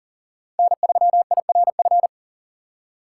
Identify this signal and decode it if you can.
Morse code, heard as a keyed tone, about 40 wpm, 700 Hz. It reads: N3IRF
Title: USA Call Signs - 5 Characters - 2x Character Spacing - 40wpm